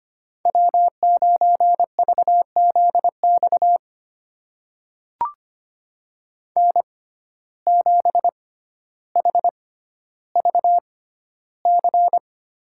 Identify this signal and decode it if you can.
Morse code, heard as a keyed tone, about 25 wpm, spaced as for 13 wpm, 700 Hz. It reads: W9VZX N7HVC
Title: USA Call Signs - 5 Characters - 6x Character Spacing - 25wpm